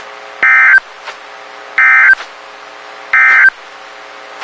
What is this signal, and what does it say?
Emergency Alert System (SAME) tones, heard as harsh buzzing bursts. It is heard from 0.3 to 3.6 s.
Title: WXK91 NOAA Weather Radio